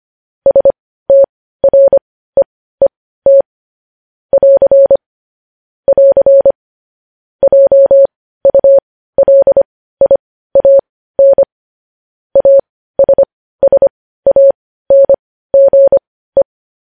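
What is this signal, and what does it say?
Morse code, heard as a keyed tone, about 25 wpm, spaced as for 15 wpm, 550 Hz. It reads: STREET <AR> <AR> JULIAN ASSANGE
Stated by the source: Morse code news headlines at 15 wpm.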